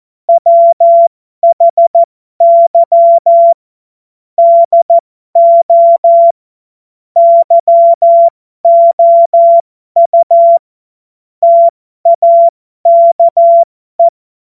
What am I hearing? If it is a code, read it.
Morse code, heard as a keyed tone, about 14 wpm, spaced as for 12 wpm, 680 Hz. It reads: WHY DO YOU TAKE
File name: Morsekurs / Witze